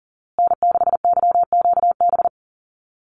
Morse code, heard as a keyed tone, about 40 wpm, 700 Hz. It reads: N6YQB
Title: USA Call Signs Repeated 2x 40wpm